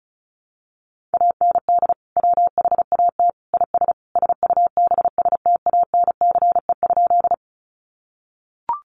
Morse code, heard as a keyed tone, about 34 wpm, 700 Hz. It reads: AND WHAT IS SUBSTANCE?
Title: Sentences from Top 1500 Words 35wpm